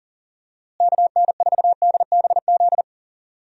Morse code, heard as a keyed tone, about 40 wpm, 700 Hz. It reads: KN4DBZ
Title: USA Call Signs - 6 Characters - Repeated 2x - 40wpm